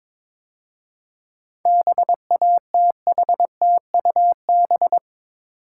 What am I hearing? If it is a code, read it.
Morse code, heard as a keyed tone, about 27 wpm, 700 Hz. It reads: BATHTUB
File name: Easy Words - Speed-Racing with Hints - 22wpm